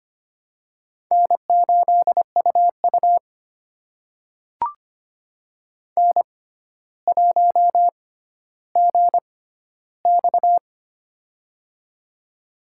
Morse code, heard as a keyed tone, about 25 wpm, spaced as for 13 wpm, 700 Hz. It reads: N8UU N1GX
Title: USA Call Signs - 4 Characters - 6x Character Spacing - 25wpm